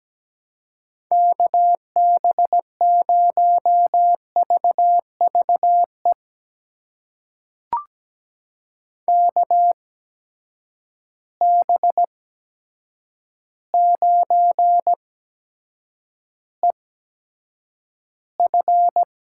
Morse code, heard as a keyed tone, about 17 wpm, spaced as for 8 wpm, 700 Hz. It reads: KB0VVE KB9EF
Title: USA Call Signs - 8x Character Spacing - 17wpm